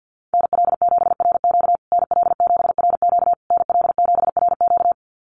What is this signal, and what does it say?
Morse code, heard as a keyed tone, about 50 wpm, 700 Hz. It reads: NL7RQ NL7RQ NL7RQ
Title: USA Call Signs Repeated 3x 50wpm